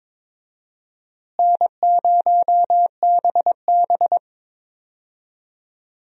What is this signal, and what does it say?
Morse code, heard as a keyed tone, about 22 wpm, 700 Hz. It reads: N0BB
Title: USA Call Signs - 4 Characters - Repeated 3x - 22wpm